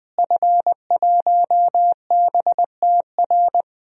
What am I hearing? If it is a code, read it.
Morse code, heard as a keyed tone, about 21 wpm, 700 Hz. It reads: F1BTR
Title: DXCC-100 Call Signs - 5 Characters - Repeated 3x - 20wpm